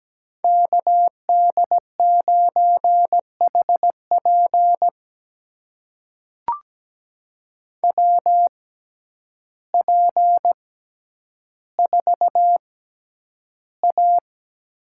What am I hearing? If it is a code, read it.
Morse code, heard as a keyed tone, about 17 wpm, spaced as for 9 wpm, 700 Hz. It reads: KD9HP WP4A
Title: USA Call Signs - 5 Characters - 6x Character Spacing - 17wpm